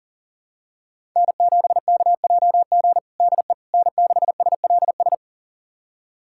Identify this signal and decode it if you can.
Morse code, heard as a keyed tone, about 40 wpm, 700 Hz. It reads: N7KJG DE N6SLS
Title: USA Callsign DE Callsign - 5 Characters - 40wpm